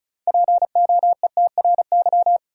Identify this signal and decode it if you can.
Morse code, heard as a keyed tone, about 36 wpm, 700 Hz. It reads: POETRY
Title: Six Letter Words - 35wpm